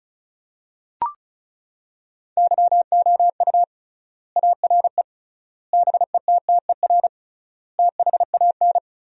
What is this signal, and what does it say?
Morse code, heard as a keyed tone, about 34 wpm, 700 Hz. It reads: YOU ARE BETTER THAN
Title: 3x Word-Spacing - Sentences from Top 400 Words - 35wpm